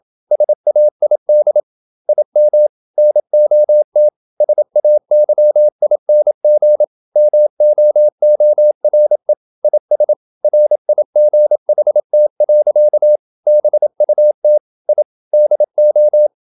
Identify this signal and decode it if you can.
Morse code, heard as a keyed tone, about 28 wpm, 600 Hz. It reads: SAID IM NOT SAYING MOORE IS RIGHT. BUT I DO
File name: Never 0027wpm